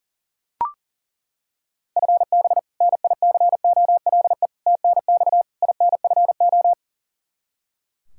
Morse code, heard as a keyed tone, about 40 wpm, 700 Hz. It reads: FB NICOLE TNX INFO
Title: Practice QSO - Exchange 3 - More Info - 40wpm